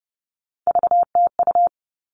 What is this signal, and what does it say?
Morse code, heard as a keyed tone, about 30 wpm, 700 Hz. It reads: VTU
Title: All Cut Numbers - 3-Digit 30wpm